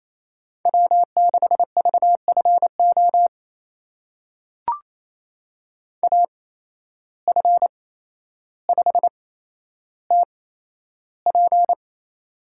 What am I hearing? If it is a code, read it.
Morse code, heard as a keyed tone, about 28 wpm, spaced as for 12 wpm, 700 Hz. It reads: W6VFO AF5TP
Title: USA Call Signs - 5 Characters - 8x Character Spacing - 28wpm